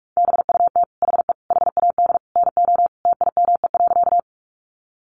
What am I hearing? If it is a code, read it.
Morse code, heard as a keyed tone, about 44 wpm, 700 Hz. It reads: BUT HE HAD NO TIME.
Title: Sentences from Top 100 Words 45wpm